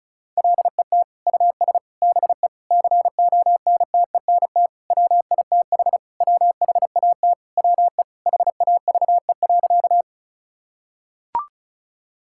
Morse code, heard as a keyed tone, about 34 wpm, 700 Hz. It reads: LET US BE CONTENT WITH WHAT WE HAVE.
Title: Sentences from Top 1000 Words - Review Part 6 - 35wpm